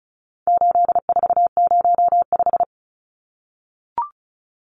Morse code, heard as a keyed tone, about 34 wpm, 700 Hz. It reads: 8405
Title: Numbers - 4-Digit 35wpm